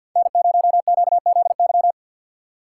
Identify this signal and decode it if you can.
Morse code, heard as a keyed tone, about 48 wpm, 700 Hz. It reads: N0QZY
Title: USA Call Signs - 5 Characters - 2x Character Spacing - 50wpm